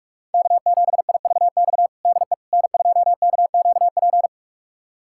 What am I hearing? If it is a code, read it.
Morse code, heard as a keyed tone, about 47 wpm, 700 Hz. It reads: K7IVX DE N2KQP
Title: USA Callsign DE Callsign - 5 Characters - 45wpm